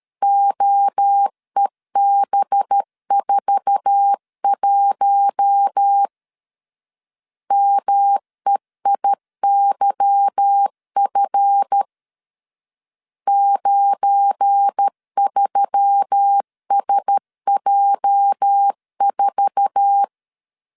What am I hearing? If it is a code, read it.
Morse code, heard as a keyed tone, about 13 wpm, 800 Hz. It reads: OEB41 MEIYF 93SJ4
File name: LEZIONI / 8 YJ45 / 8 YJ4 12wpm 2151